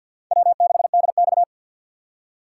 Morse code, heard as a keyed tone, about 47 wpm, 700 Hz. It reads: W6DX
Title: USA Call Signs - 2x Character Spacing - 50wpm